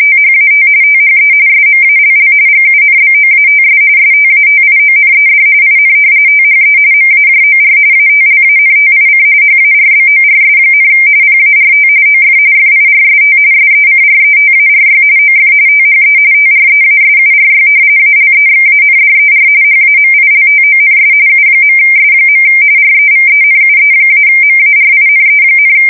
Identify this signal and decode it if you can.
RTTY, about 45.45 baud. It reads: THAT THE SMALL GUNBOATS STATIONED IN THE MAJOR ATLANTIC HARBORS, PART OF FIRST SECRETARY OF THE TREASURY ALEXANDER HAMILTONS U.S. REVENUE-MARINE,
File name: fsk170 / usnavy1